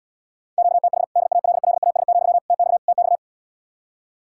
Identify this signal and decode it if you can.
Morse code, heard as a keyed tone, about 77 wpm, 700 Hz. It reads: QTH DALLAS, TX TX